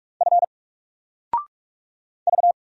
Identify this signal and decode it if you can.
Morse code, heard as a keyed tone, about 46 wpm, 700 Hz. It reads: F V
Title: Single Letter-Number 45wpm